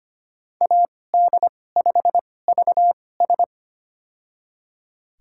Morse code, heard as a keyed tone, about 25 wpm, spaced as for 18 wpm, 700 Hz. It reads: AD5VS
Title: USA Call Signs - 5 Characters - 2x Character Spacing - 25wpm